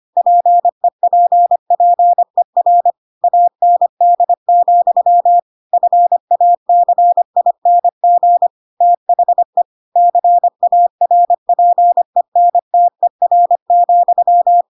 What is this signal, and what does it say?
Morse code, heard as a keyed tone, about 25 wpm, 700 Hz. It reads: PEPPER AND, FACING THE CARPENTER,
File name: Pinocchio / Pinocchio ch2 25 25 ik1hge